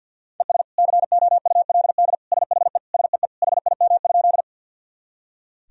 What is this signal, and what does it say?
Morse code, heard as a keyed tone, about 46 wpm, 700 Hz. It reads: ES COULD SHE SEE HIM?
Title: Sentences from Top 100 Words and Abbreviations 50wpm